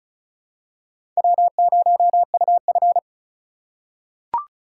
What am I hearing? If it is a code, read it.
Morse code, heard as a keyed tone, about 36 wpm, 700 Hz. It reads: W0UF
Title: K1USN SST QSO Practice - 35wpm